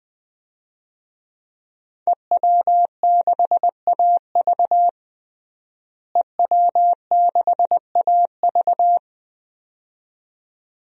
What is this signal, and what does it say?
Morse code, heard as a keyed tone, about 20 wpm, 700 Hz. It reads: EW6AV EW6AV
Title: DXCC-100 Call Signs - 5 Characters - Repeated 2x - 20wpm